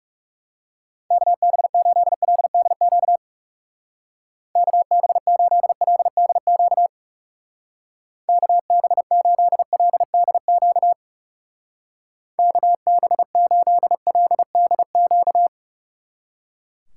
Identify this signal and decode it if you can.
Morse code, heard as a keyed tone, about 37 wpm, 700 Hz. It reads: KB8LDQ KB8LDQ KB8LDQ KB8LDQ
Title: Speed Racing - USA Call Signs 30wpm